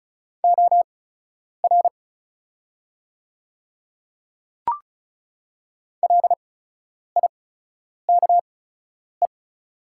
Morse code, heard as a keyed tone, about 36 wpm, spaced as for 10 wpm, 700 Hz. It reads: OR LIKE
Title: 8x Character Spacing - Top 100 Words - 35wpm